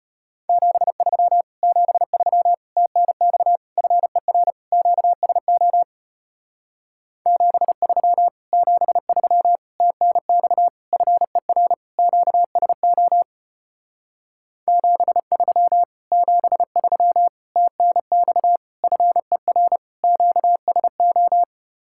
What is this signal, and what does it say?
Morse code, heard as a keyed tone, about 33 wpm, 700 Hz. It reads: 73 73 TNX FER QSO 73 73 TNX FER QSO 73 73 TNX FER QSO